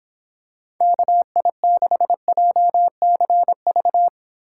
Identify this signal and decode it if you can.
Morse code, heard as a keyed tone, about 26 wpm, 700 Hz. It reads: KI6JCV